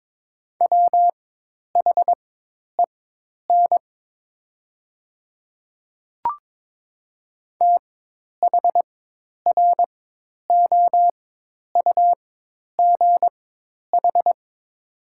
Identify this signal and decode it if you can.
Morse code, heard as a keyed tone, about 22 wpm, spaced as for 10 wpm, 700 Hz. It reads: WHEN THROUGH
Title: 4x Character Spacing - Top 200 Words - 22wpm